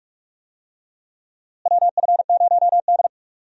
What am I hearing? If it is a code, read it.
Morse code, heard as a keyed tone, about 46 wpm, 700 Hz. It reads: WF0D
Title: USA Call Signs - 4 Characters - Repeated 3x - 45wpm